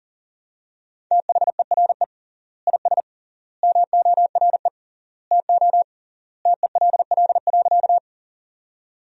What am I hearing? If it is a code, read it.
Morse code, heard as a keyed tone, about 40 wpm, 700 Hz. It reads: THERE IS MORE TO TELL.
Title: 3x Word-Spacing - Sentences from Top 200 Words - 40wpm